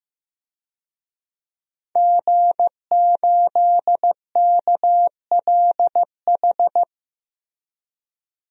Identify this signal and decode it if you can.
Morse code, heard as a keyed tone, about 15 wpm, 700 Hz. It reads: G8KLH
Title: DXCC-100 Call Signs - 5 Characters - Repeated 3x - 15wpm